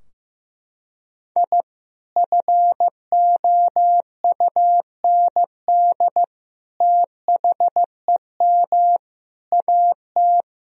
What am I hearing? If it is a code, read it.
Morse code, heard as a keyed tone, about 15 wpm, 700 Hz. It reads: I FOUND THEM AT
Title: Sentences from Top 1000 Words - Review Part 2 - 15wpm